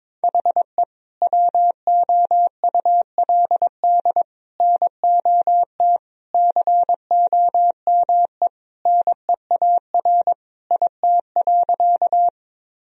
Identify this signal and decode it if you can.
Morse code, heard as a keyed tone, about 22 wpm, 700 Hz. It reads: HE WOULD NOT COME NEAR IT.
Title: Sentences from Top 1000 Words - Review Part 1 - 22wpm